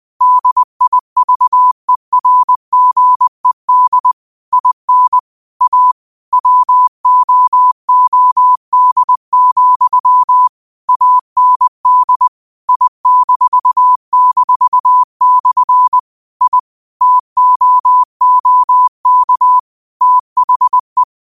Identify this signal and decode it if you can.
Morse code, heard as a keyed tone, about 21 wpm, 1000 Hz. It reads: DIVERGED IN A WOOD, AND I--/ I TOOK THE
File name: qotdfiles / 20wpm